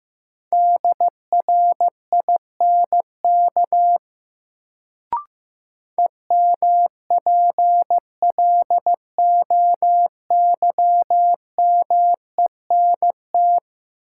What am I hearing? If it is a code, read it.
Morse code, heard as a keyed tone, about 15 wpm, 700 Hz. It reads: DRINK EMPLOYMENT
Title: Sentences from Top 1500 Words - New Word Drill - 15wpm